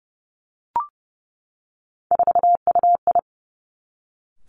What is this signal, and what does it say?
Morse code, heard as a keyed tone, about 30 wpm, 700 Hz. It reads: 4UI
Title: Three Challenging Letter-Number Combinations 30wpm